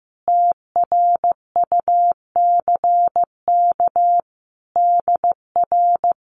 Morse code, heard as a keyed tone, about 15 wpm, 700 Hz. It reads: TRUCK DR
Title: Common QSO Phrases Encore 15wpm